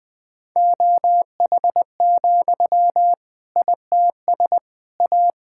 Morse code, heard as a keyed tone, about 20 wpm, 700 Hz. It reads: OH, ITS A
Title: Sentences from Top 1000 Words - Review Part 5 - 20wpm